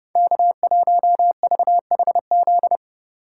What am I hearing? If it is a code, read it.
Morse code, heard as a keyed tone, about 32 wpm, 700 Hz. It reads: K1VHZ
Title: USA Call Signs - 3x Character Spacing - 30wpm